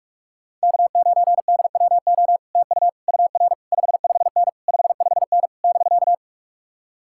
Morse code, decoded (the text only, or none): K9DWO TU UR 55N 55N <BK>